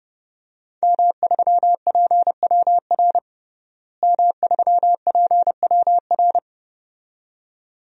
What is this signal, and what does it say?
Morse code, heard as a keyed tone, about 30 wpm, 700 Hz. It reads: M3PWR M3PWR
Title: DXCC-100 Call Signs - 5 Characters - Repeated 2x - 30wpm